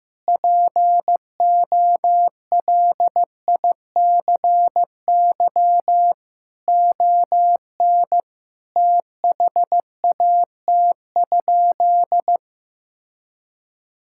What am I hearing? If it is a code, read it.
Morse code, heard as a keyed tone, about 15 wpm, 700 Hz. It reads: POLICY ON THAT?
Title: Sentences from Top 1000 Words - Review Part 5 - 15wpm